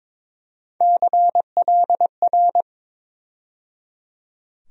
Morse code, heard as a keyed tone, about 22 wpm, 700 Hz. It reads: CLR
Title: USA Call Sign Postfixes - 3 Letters 22wpm